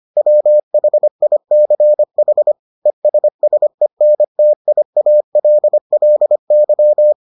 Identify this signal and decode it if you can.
Morse code, heard as a keyed tone, about 25 wpm, 600 Hz. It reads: WHICH ESSENTIALLY